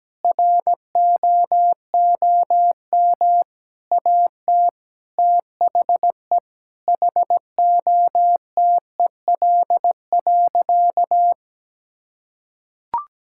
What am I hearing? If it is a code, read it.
Morse code, heard as a keyed tone, about 17 wpm, 700 Hz. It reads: ROOM AT THE HOTEL.
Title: Sentences from Top 1600 Words 17wpm